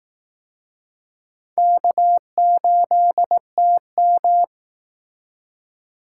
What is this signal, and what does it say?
Morse code, heard as a keyed tone, about 19 wpm, 700 Hz. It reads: K8TM
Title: Speed Racing - USA 1x2 Call Signs 15wpm